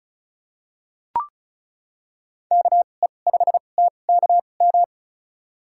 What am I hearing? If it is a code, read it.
Morse code, heard as a keyed tone, about 34 wpm, spaced as for 25 wpm, 700 Hz. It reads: KE5TKM
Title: USA Call Signs - 6 Characters - 2x Character Spacing - 35wpm